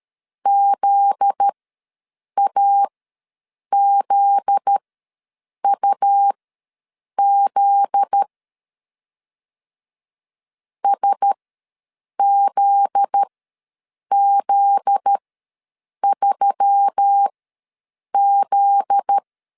Morse code, heard as a keyed tone, about 13 wpm, spaced as for 7 wpm, 800 Hz. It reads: ZAZUZ SZZ3Z
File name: LEZIONI / 7 GF9Z / 7 GF9Z ws 12wpm 3321